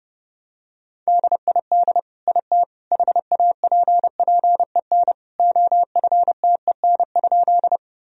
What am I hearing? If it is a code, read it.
Morse code, heard as a keyed tone, about 30 wpm, 700 Hz. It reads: DID IT HAPPEN OFTEN?
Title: Sentences from Top 1000 Words - Review Part 1 - 30wpm